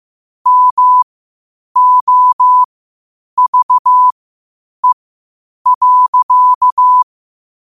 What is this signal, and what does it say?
Morse code, heard as a keyed tone, about 15 wpm, spaced as for 8 wpm, 1000 Hz. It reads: MOVE.
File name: qotdfiles / 8wpm